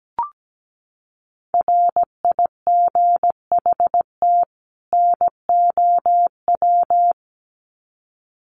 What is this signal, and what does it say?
Morse code, heard as a keyed tone, about 17 wpm, 700 Hz. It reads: RIGHT NOW
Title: Sets of 2 Words 17wpm